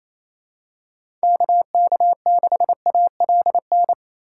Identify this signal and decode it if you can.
Morse code, heard as a keyed tone, about 28 wpm, 700 Hz. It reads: KK6ALN
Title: USA Call Signs - 6 Characters - 3x Character Spacing - 28wpm